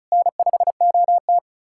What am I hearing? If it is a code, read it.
Morse code, heard as a keyed tone, about 48 wpm, spaced as for 42 wpm, 700 Hz. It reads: N5OT
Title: Speed-Racing Common QSO Phrases - 35wpm